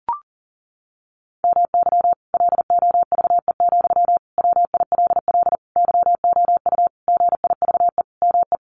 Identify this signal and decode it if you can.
Morse code, heard as a keyed tone, about 40 wpm, 700 Hz. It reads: MY LOVE, WILL YOU GIVE ME
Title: Sentences from Top 1200 Words 40wpm